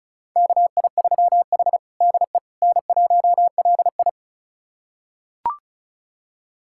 Morse code, heard as a keyed tone, about 36 wpm, 700 Hz. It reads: KI3H DE N1LI
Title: USA Callsign DE Callsign - 4 Characters - 35wpm